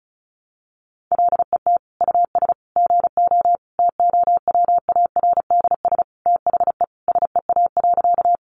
Morse code, heard as a keyed tone, about 34 wpm, 700 Hz. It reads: LET US GO TOWARDS THE SEA.
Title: Sentences from Top 1100 Words 35wpm